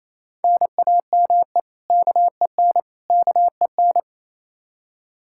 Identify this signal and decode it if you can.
Morse code, heard as a keyed tone, about 28 wpm, 700 Hz. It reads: NAME KEN KEN
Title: Practice QSO - Exchange 1 - Beginning - 28wpm